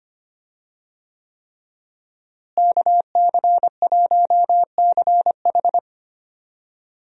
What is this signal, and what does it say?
Morse code, heard as a keyed tone, about 25 wpm, 700 Hz. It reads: KC1CH
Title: USA Call Signs - 5 Characters - Repeated 3x - 25wpm